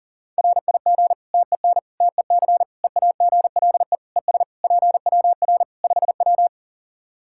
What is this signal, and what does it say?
Morse code, heard as a keyed tone, about 37 wpm, 700 Hz. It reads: RIG TEN TEC EAGLE ES PWR 5W
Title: Practice QSO - Exchange 2 - Info - 40wpm